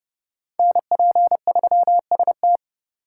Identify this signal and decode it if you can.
Morse code, heard as a keyed tone, about 37 wpm, 700 Hz. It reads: NP3ST
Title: Speed Racing - USA 2x2 Call Signs 30wpm